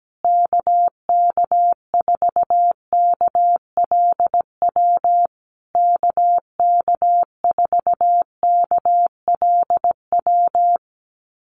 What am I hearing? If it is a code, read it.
Morse code, heard as a keyed tone, about 17 wpm, 700 Hz. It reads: KK4KLW KK4KLW
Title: USA Call Signs Repeated 3x 17wpm